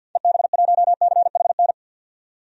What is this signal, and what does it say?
Morse code, heard as a keyed tone, about 52 wpm, 700 Hz. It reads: EB1CHN
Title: DXCC-100 Call Signs - 6 Characters - 50wpm